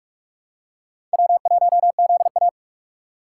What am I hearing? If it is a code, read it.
Morse code, heard as a keyed tone, about 46 wpm, 700 Hz. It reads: W1ZA
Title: USA Call Signs - 3x Character Spacing - 45wpm